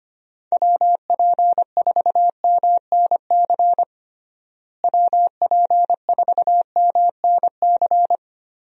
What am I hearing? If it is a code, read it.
Morse code, heard as a keyed tone, about 25 wpm, 700 Hz. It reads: WP4MNC WP4MNC
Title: USA Call Signs - 6 Characters - Repeated 2x - 25wpm